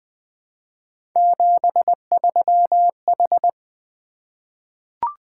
Morse code, heard as a keyed tone, about 20 wpm, 700 Hz. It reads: 73H
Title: Three Letter-Number Combinations - 20wpm